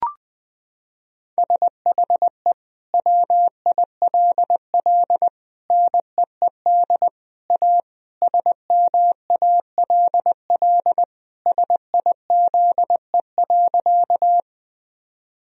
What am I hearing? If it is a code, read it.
Morse code, heard as a keyed tone, about 20 wpm, 700 Hz. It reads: SHE WILL NEED A SMALL SIZE.
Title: Sentences from Top 1000 Words - Review Part 2 - 20wpm